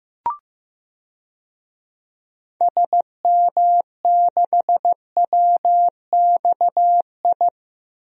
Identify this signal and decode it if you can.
Morse code, heard as a keyed tone, about 15 wpm, 700 Hz. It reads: SM6WXI
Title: DXCC-100 Call Signs - 6 Characters - Repeated 2x - 15wpm